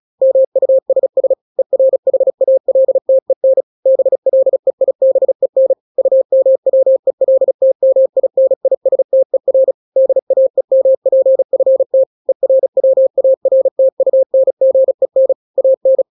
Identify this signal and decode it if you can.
Morse code, heard as a keyed tone, about 36 wpm, 530 Hz. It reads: MUSS ERHALTEN BLEIBEN UMWELTMINISTER DAEMPFT ERWARTUNGEN AN
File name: daily / 35wpm